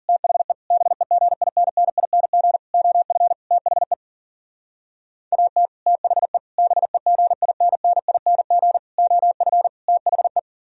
Morse code, heard as a keyed tone, about 51 wpm, 700 Hz. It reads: THE BEGINNING OF THE AT THE BEGINNING OF THE